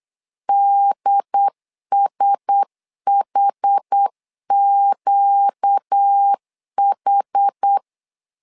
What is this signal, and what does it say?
Morse code, heard as a keyed tone, about 8 wpm, 800 Hz. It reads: DSHQH
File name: LEZIONI / 5 LBVC / 5 LB 8wpm 2816